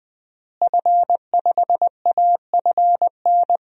Morse code, heard as a keyed tone, about 20 wpm, 700 Hz. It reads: F5AFN
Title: DXCC-100 Call Signs - 5 Characters - Repeated 3x - 20wpm